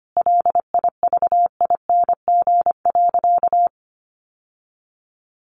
Morse code, heard as a keyed tone, about 25 wpm, 700 Hz. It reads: LIVING.
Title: Sentences from Top 1500 Words 25wpm